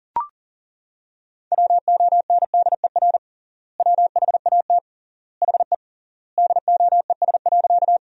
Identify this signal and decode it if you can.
Morse code, heard as a keyed tone, about 37 wpm, 700 Hz. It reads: WONDER WHAT HE DOES.
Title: 3x Word-Spacing - Sentences from Top 500 Words - 40wpm